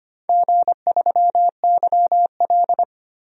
Morse code, heard as a keyed tone, about 25 wpm, 700 Hz. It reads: G3YL
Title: DXCC-100 Call Signs - 4 Characters - 25wpm